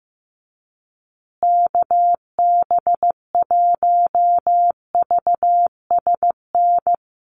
Morse code, heard as a keyed tone, about 15 wpm, 700 Hz. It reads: KB1VSN
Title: USA Call Signs Repeated 2x 15wpm